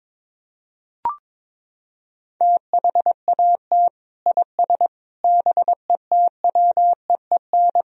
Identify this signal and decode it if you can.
Morse code, heard as a keyed tone, about 21 wpm, 700 Hz. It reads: THAT IS BETWEEN
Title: Sentences from Top 1000 Words - Review Part 4 - 22wpm